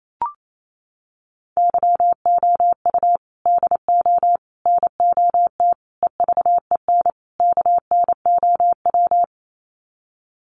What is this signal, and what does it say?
Morse code, heard as a keyed tone, about 28 wpm, 700 Hz. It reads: YOU DO NOT EVEN KNOW
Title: Sets of 5 Words Encore2 28wpm